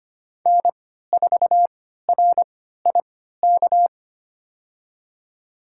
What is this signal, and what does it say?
Morse code, heard as a keyed tone, about 25 wpm, spaced as for 14 wpm, 700 Hz. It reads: N4RIK
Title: USA Call Signs - 5 Characters - 3x Character Spacing - 25wpm